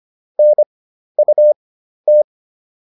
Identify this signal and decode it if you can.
Morse code, heard as a keyed tone, about 25 wpm, spaced as for 12 wpm, 600 Hz. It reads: NUT